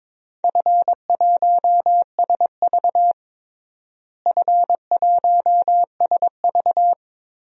Morse code, heard as a keyed tone, about 22 wpm, 700 Hz. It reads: F1SV F1SV
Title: DXCC-100 Call Signs - 4 Characters - Repeated 2x - 22wpm